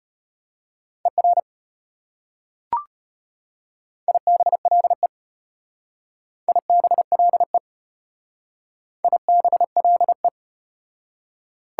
Morse code, heard as a keyed tone, about 33 wpm, 690 Hz. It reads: ER IBLE IBLE IBLE